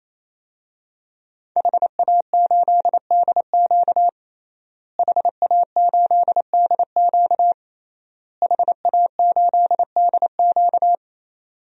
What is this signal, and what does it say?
Morse code, heard as a keyed tone, about 28 wpm, 700 Hz. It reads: HA8DQ HA8DQ HA8DQ
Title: DXCC-100 Call Signs - 5 Characters - Repeated 3x - 28wpm